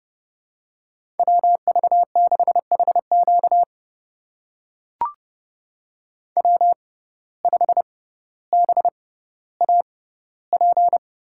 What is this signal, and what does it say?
Morse code, heard as a keyed tone, about 30 wpm, spaced as for 15 wpm, 700 Hz. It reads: WV6HQ W5BAP
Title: USA Call Signs - 5 Characters - 6x Character Spacing - 30wpm